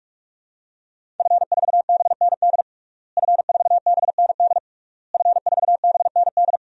Morse code, heard as a keyed tone, about 46 wpm, 700 Hz. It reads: F4BND F4BND F4BND
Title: DXCC-100 Call Signs - 5 Characters - Repeated 3x - 45wpm